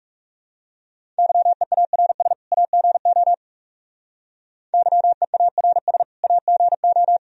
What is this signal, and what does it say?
Morse code, heard as a keyed tone, about 35 wpm, 700 Hz. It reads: YEARS AGO YEARS AGO
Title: Speed-Racing - Sets of 2 Words 30wpm